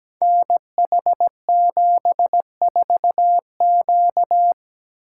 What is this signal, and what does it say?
Morse code, heard as a keyed tone, about 17 wpm, 700 Hz. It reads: NH74Q
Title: Code Groups - 5 Letter:Numbers - 17wpm